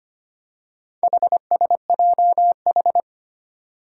Morse code, heard as a keyed tone, about 25 wpm, 700 Hz. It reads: HSJH
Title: Four Challenging Letter-Number Combinations 25wpm